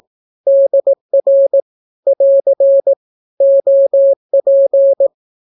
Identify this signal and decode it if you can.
Morse code, heard as a keyed tone, about 18 wpm, 550 Hz. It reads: DR <AR> OP
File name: naukapodstaw / skroty / skroty